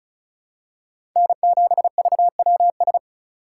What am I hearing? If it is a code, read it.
Morse code, heard as a keyed tone, about 36 wpm, 700 Hz. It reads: N7VWS
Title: USA Call Signs - 5 Characters - 3x Character Spacing - 35wpm